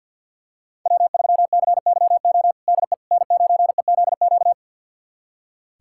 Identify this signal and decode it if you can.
Morse code, heard as a keyed tone, about 46 wpm, 700 Hz. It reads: W3CYO DE N9EZQ
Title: USA Callsign DE Callsign - 5 Characters - 50wpm